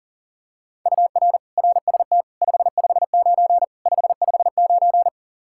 Morse code, heard as a keyed tone, about 40 wpm, 700 Hz. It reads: UR RST 559 559
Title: K1USN SST QSO Practice 40wpm